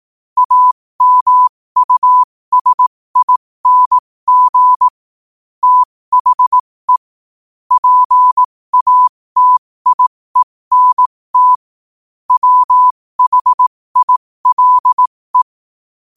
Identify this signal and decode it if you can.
Morse code, heard as a keyed tone, about 18 wpm, spaced as for 15 wpm, 1000 Hz. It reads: AMUSING THE PATIENT WHILE
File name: qotdfiles / 15wpm